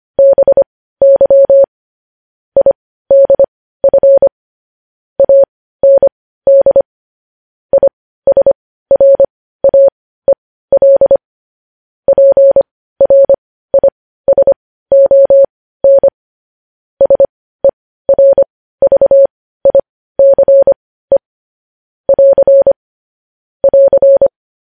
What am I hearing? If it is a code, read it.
Morse code, heard as a keyed tone, about 25 wpm, spaced as for 15 wpm, 550 Hz. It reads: BY IDF AND ISRAEL PRISON SERVICE <AR> <AR>
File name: podcast / News headlines-1x-25x15